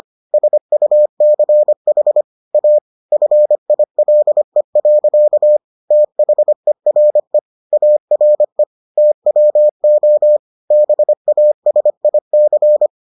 Morse code, heard as a keyed tone, about 25 wpm, 600 Hz. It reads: SUCH A FILE. THERE ARE TWO BASIC